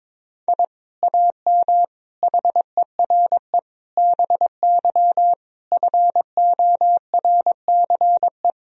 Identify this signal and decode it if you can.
Morse code, heard as a keyed tone, about 23 wpm, 700 Hz. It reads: I AM HERE BY FORCE
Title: Sentences from Top 1000 Words - Review Part 3 - 22wpm